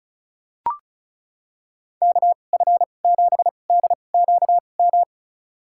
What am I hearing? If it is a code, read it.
Morse code, heard as a keyed tone, about 36 wpm, spaced as for 25 wpm, 700 Hz. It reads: KF7DQM
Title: USA Call Signs - 2x Character Spacing - 35wpm